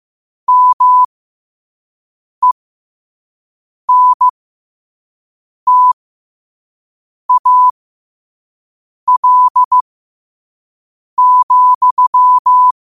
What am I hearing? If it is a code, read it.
Morse code, heard as a keyed tone, about 15 wpm, spaced as for 5 wpm, 1000 Hz. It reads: MENTAL,
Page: Quotes for Thu, 14 Aug 2025 in Morse Code at 5 words per minute.